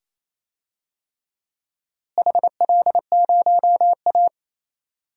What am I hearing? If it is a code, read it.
Morse code, heard as a keyed tone, about 28 wpm, 700 Hz. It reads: HL0A
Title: DXCC-100 Call Signs - 4 Characters - Repeated 2x - 28wpm